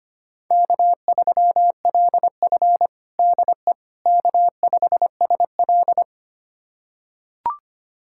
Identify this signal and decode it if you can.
Morse code, heard as a keyed tone, about 25 wpm, 700 Hz. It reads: K3LF DE K5SL
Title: USA Callsign DE Callsign - 4 Characters - 25wpm